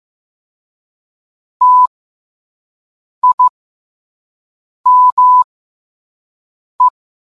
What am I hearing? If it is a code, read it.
Morse code, heard as a keyed tone, about 15 wpm, spaced as for 5 wpm, 1000 Hz. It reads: TIME